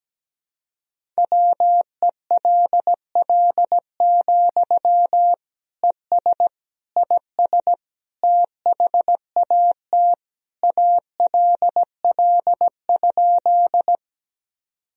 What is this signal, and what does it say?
Morse code, heard as a keyed tone, about 17 wpm, 700 Hz. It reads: WELL, ES IS THAT ALL?
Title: Sentences from Top 200 Words and Abbreviations 17wpm